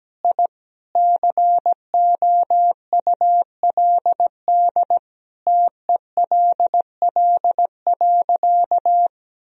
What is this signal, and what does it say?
Morse code, heard as a keyed tone, about 17 wpm, 700 Hz. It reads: I COULD TELL.
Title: Sentences from Top 1000 Words - Review Part 4 - 17wpm